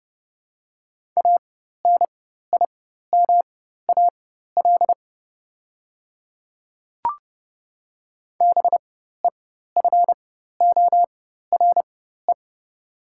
Morse code, heard as a keyed tone, about 30 wpm, spaced as for 14 wpm, 700 Hz. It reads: ANIMAL BEFORE
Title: 4x Character Spacing - Top 200 Words - 30wpm